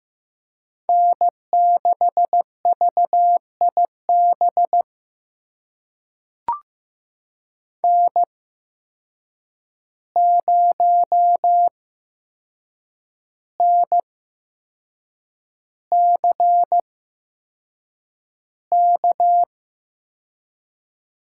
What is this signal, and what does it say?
Morse code, heard as a keyed tone, about 15 wpm, spaced as for 6 wpm, 700 Hz. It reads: N6VIB N0NCK
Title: USA Call Signs - 5 Characters - 8x Character Spacing - 15wpm